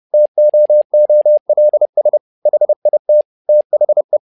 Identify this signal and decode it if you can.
Morse code, heard as a keyed tone, about 30 wpm, 600 Hz. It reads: TOOLS HIT THE
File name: learning center / tower trouble / tower trouble 30wpm